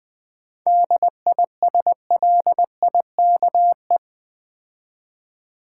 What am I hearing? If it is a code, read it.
Morse code, heard as a keyed tone, about 20 wpm, 700 Hz. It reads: DISLIKE
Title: 7 Letter Words - First 3 Letters Given - 20wpm